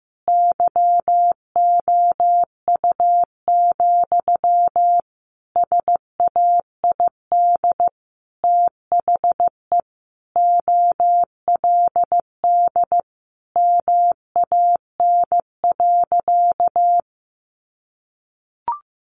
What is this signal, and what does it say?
Morse code, heard as a keyed tone, about 15 wpm, 700 Hz. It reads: YOU, SAID THE OLD MAN.
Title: Sentences from Top 1500 Words 15wpm